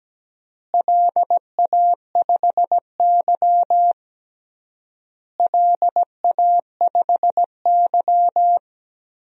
Morse code, heard as a keyed tone, about 17 wpm, 700 Hz. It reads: LA5Y LA5Y
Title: DXCC-100 Call Signs - 4 Characters - Repeated 3x - 17wpm